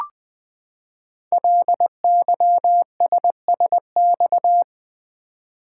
Morse code, heard as a keyed tone, about 20 wpm, 700 Hz. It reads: LYSSX
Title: Code Groups - 5 Letters - 20wpm